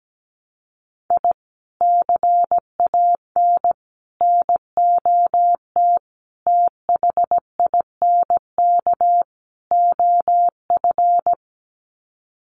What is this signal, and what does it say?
Morse code, heard as a keyed tone, about 17 wpm, 700 Hz. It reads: I CAN NOT THINK OF
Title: Sets of 5 Words 17wpm